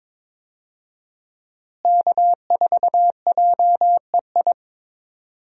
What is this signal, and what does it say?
Morse code, heard as a keyed tone, about 22 wpm, 700 Hz. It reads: K4JEI
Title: USA Call Signs - 5 Characters - Repeated 2x - 22wpm